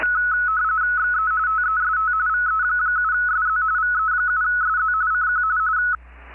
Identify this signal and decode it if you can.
RTTY, about 45.45 baud. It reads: XE TEST VE7SZ VE7SZ CQ
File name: 20m-rtty-1